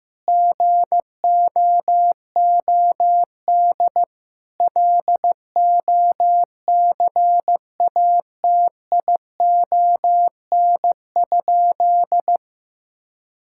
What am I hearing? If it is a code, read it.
Morse code, heard as a keyed tone, about 15 wpm, 700 Hz. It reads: GOOD LOCATION?
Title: Sentences from Top 1000 Words - Review Part 6 - 15wpm